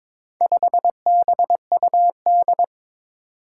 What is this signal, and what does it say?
Morse code, heard as a keyed tone, about 23 wpm, 700 Hz. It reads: 5BUD
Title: Four Challenging Letter-Number Combinations 22wpm